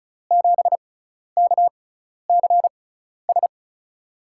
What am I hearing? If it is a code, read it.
Morse code, heard as a keyed tone, about 34 wpm, spaced as for 12 wpm, 700 Hz. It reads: 7KCS
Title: USA Call Signs - 5 Characters - 6x Character Spacing - 35wpm